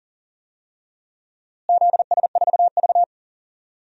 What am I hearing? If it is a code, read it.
Morse code, heard as a keyed tone, about 40 wpm, 700 Hz. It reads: ZS4V
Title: DXCC-100 Call Signs - 4 Characters - Repeated 3x - 40wpm